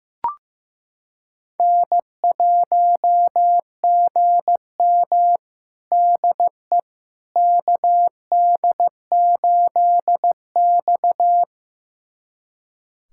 Morse code, heard as a keyed tone, about 15 wpm, 700 Hz. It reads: N1GM DE KD8X
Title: USA Callsign DE Callsign - 4 Characters - 15wpm